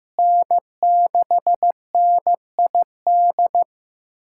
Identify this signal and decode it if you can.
Morse code, heard as a keyed tone, about 16 wpm, 700 Hz. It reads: N6NID
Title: USA Callsign DE Callsign - 5 Characters - 15wpm